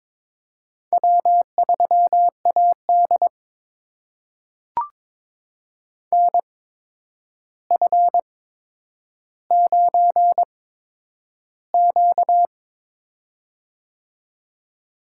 Morse code, heard as a keyed tone, about 22 wpm, spaced as for 9 wpm, 700 Hz. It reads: W3AD NF9Q
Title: USA Call Signs - 8x Character Spacing - 22wpm